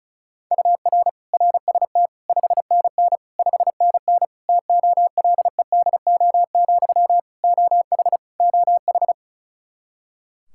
Morse code, heard as a keyed tone, about 36 wpm, 700 Hz. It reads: UR RST 5NN 5NN TOLEDO, OH OH
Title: Practice QSO - Exchange 1 - Beginning - 35wpm